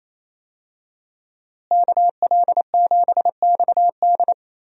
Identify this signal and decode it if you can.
Morse code, heard as a keyed tone, about 28 wpm, 700 Hz. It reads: KL7XD
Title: DXCC-100 Call Signs - 5 Characters - Repeated 3x - 28wpm